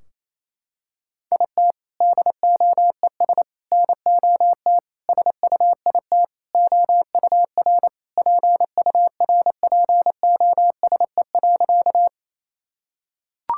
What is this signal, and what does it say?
Morse code, heard as a keyed tone, about 28 wpm, 700 Hz. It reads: IT DOES NOT SUIT OUR PURPOSE.
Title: Sentences from Top 1600 Words 28wpm